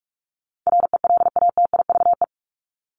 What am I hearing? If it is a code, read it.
Morse code, heard as a keyed tone, about 44 wpm, 700 Hz. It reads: RELATIVE
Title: LY Suffix - Top 100 Words with Base Words 45wpm